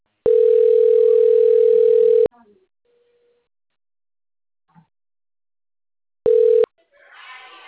Ringback tone (North America), 440 + 480 Hz.